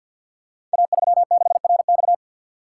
Morse code, heard as a keyed tone, about 50 wpm, 690 Hz. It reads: A36RX
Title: Code Groups - 5 Letter:Numbers - 50wpm